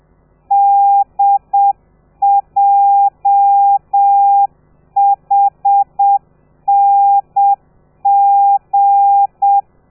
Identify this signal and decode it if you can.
Morse code, heard as a keyed tone, about 7 wpm, 800 Hz. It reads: DJHNG
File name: telegrafia / i2tfj / Lez13 30